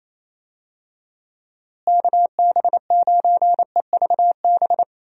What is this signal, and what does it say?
Morse code, heard as a keyed tone, about 28 wpm, 700 Hz. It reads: KB9EVB
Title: USA Call Signs - 6 Characters - Repeated 3x - 28wpm